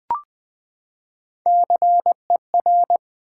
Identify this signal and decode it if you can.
Morse code, heard as a keyed tone, about 20 wpm, 700 Hz. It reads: CER
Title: Build A Word - 3 Letters At A Time - 9-Letter Words - 20wpm